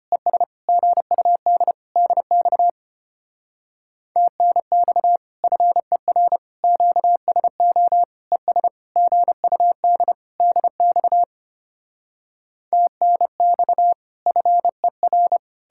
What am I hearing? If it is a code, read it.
Morse code, heard as a keyed tone, about 33 wpm, 700 Hz. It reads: ES GUD DX TNX FER QSO ES GUD DX TNX FER